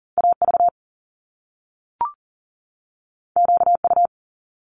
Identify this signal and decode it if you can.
Morse code, heard as a keyed tone, about 39 wpm, 700 Hz. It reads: AV QU
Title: Top Two Letter Combinations in Words 40wpm